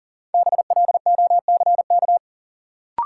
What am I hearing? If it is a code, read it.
Morse code, heard as a keyed tone, about 40 wpm, 700 Hz. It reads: BLOCK
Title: Sentences from Top 1600 Words - New Word Drill - 40wpm